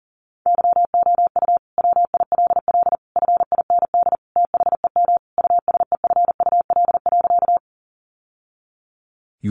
Morse code, heard as a keyed tone, about 40 wpm, 700 Hz. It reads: YOU WILL FIND THEM USEFUL.
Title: Sentences from Top 700 Words 40wpm